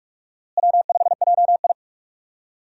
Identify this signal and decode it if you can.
Morse code, heard as a keyed tone, about 46 wpm, 700 Hz. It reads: W5JI
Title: USA Call Signs - 4 Characters - Repeated 3x - 45wpm